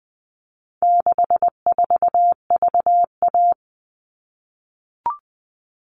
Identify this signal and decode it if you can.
Morse code, heard as a keyed tone, about 20 wpm, 700 Hz. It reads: 64VA
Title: All Cut Numbers - 4-Digit 20wpm